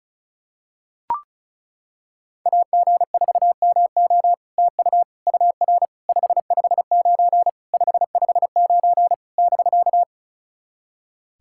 Morse code, heard as a keyed tone, about 36 wpm, 700 Hz. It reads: AG4MO TU UR 559 559 <BK>
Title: SOTA QSO Practice 35wpm